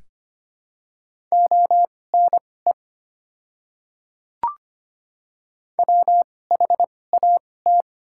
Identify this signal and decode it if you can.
Morse code, heard as a keyed tone, about 25 wpm, spaced as for 18 wpm, 700 Hz. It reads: ONE WHAT
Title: 2x Character Spacing - Top 100 Words - 25wpm